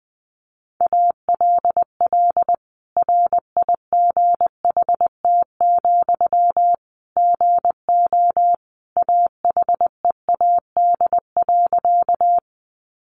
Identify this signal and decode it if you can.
Morse code, heard as a keyed tone, about 20 wpm, 700 Hz. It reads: ALL RIGHT, GO AHEAD.
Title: Sentences from Top 1100 Words 20wpm